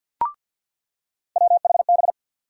Morse code, heard as a keyed tone, about 50 wpm, 700 Hz. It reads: WHB
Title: USA Call Sign Postfixes - 3 Letters 50wpm